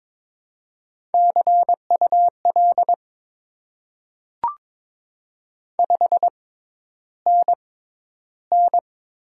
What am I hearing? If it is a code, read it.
Morse code, heard as a keyed tone, about 22 wpm, spaced as for 11 wpm, 700 Hz. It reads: CUL 5NN
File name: Top 90 QSO Elements - 6x Character Spacing - 22wpm